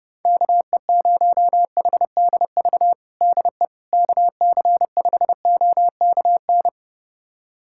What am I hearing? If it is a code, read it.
Morse code, heard as a keyed tone, about 30 wpm, 700 Hz. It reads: KE0HDV DE KC5OKN
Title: USA Callsign DE Callsign 30wpm